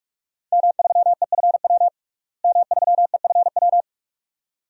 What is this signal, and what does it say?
Morse code, heard as a keyed tone, about 46 wpm, 700 Hz. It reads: M3EFW M3EFW
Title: DXCC-100 Call Signs - 5 Characters - Repeated 2x - 45wpm